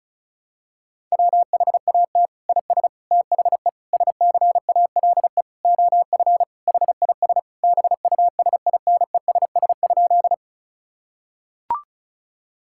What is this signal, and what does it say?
Morse code, heard as a keyed tone, about 36 wpm, 700 Hz. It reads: WHAT IS THE SCALE OF HIS BUSINESS?
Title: Sentences from Top 1000 Words - Review Part 5 - 35wpm